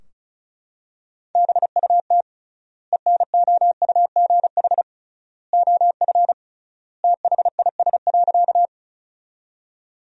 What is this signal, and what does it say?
Morse code, heard as a keyed tone, about 34 wpm, 700 Hz. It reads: BUT ENOUGH OF THIS.
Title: 3x Word-Spacing - Sentences from Top 400 Words - 35wpm